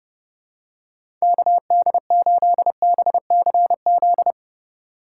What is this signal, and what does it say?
Morse code, heard as a keyed tone, about 27 wpm, 700 Hz. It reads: KD8BCZ